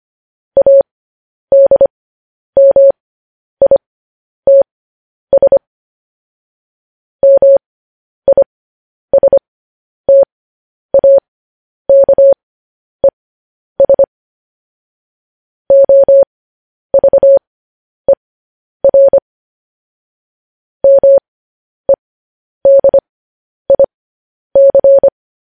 Morse code, heard as a keyed tone, about 25 wpm, spaced as for 10 wpm, 550 Hz. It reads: ADMITS MISTAKES OVER MEDIC
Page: news headlines in morse code at 10 wpm